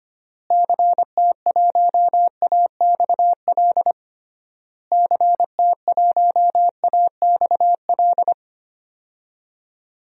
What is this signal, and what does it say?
Morse code, heard as a keyed tone, about 25 wpm, 700 Hz. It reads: CT1AXL CT1AXL
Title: DXCC-100 Call Signs - 6 Characters - Repeated 3x - 25wpm